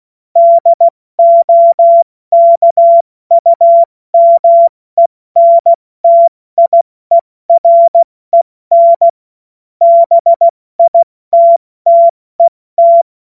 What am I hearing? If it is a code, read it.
Morse code, heard as a keyed tone, about 16 wpm, 680 Hz. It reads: DOKUMENTIEREN BITTET
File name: Morsekurs / Witze / Witz 15. August 2023 70 F80